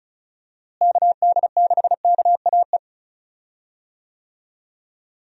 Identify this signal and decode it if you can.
Morse code, heard as a keyed tone, about 36 wpm, 700 Hz. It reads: KD6KAE
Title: USA Call Signs - 6 Characters - Repeated 2x - 35wpm